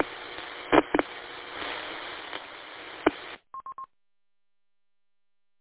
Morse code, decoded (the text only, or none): none